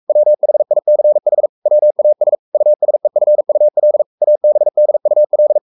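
Morse code, heard as a keyed tone, about 44 wpm, 600 Hz. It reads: WHICH WAS USEFUL ABDUL
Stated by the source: Morse code at 43 wpm sending Ken Follett's novel Never in English, from CW med Gnister.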